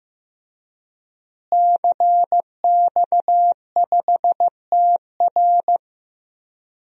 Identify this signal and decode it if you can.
Morse code, heard as a keyed tone, about 15 wpm, 700 Hz. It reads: CX5TR
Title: DXCC-100 Call Signs - 5 Characters - Repeated 3x - 15wpm